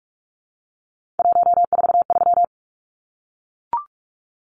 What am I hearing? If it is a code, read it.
Morse code, heard as a keyed tone, about 44 wpm, 700 Hz. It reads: 143
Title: Numbers - 3-Digit 45wpm